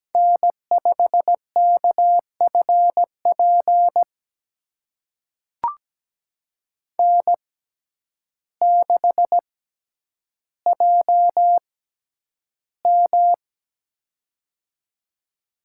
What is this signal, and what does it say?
Morse code, heard as a keyed tone, about 17 wpm, spaced as for 9 wpm, 700 Hz. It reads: N5KFP N6JM
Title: USA Call Signs - 6x Character Spacing - 17wpm